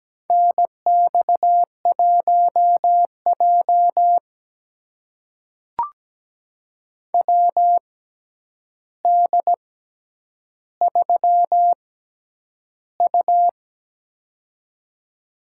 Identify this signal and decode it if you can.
Morse code, heard as a keyed tone, about 17 wpm, spaced as for 9 wpm, 700 Hz. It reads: NX1J WD3U
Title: USA Call Signs - 4 Characters - 6x Character Spacing - 17wpm